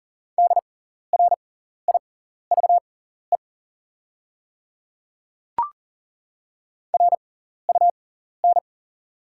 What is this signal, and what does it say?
Morse code, heard as a keyed tone, about 40 wpm, spaced as for 14 wpm, 700 Hz. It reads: DRIVE RUN
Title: 6x Character Spacing - Top 500 Words - 40wpm